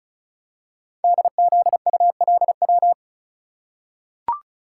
Morse code, heard as a keyed tone, about 36 wpm, 690 Hz. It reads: DZULW
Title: Code Groups - 5 Letters - 35wpm